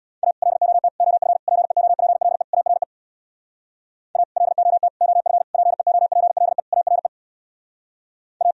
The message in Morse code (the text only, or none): A LOT OF PEOPLE ARE A LOT OF PEOPLE ARE A